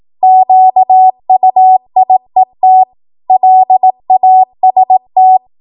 Morse code, heard as a keyed tone, about 18 wpm, 750 Hz. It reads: QUIET LAST
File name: Archive / 18 WPM / 18WPM